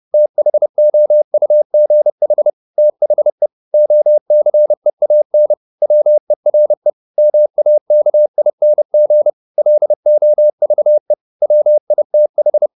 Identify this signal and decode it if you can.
Morse code, heard as a keyed tone, about 30 wpm, 600 Hz. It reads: THOUGH THE OCEAN WERE MAKING LOVE WITH